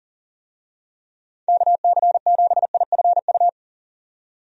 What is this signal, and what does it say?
Morse code, heard as a keyed tone, about 40 wpm, 700 Hz. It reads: KC7IFU
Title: USA Call Signs - 6 Characters - 40wpm